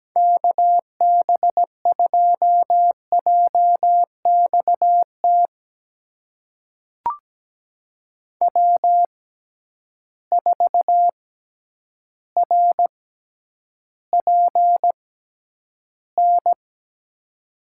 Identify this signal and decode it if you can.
Morse code, heard as a keyed tone, about 17 wpm, spaced as for 9 wpm, 700 Hz. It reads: KB2JXT W4RPN
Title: USA Call Signs - 6x Character Spacing - 17wpm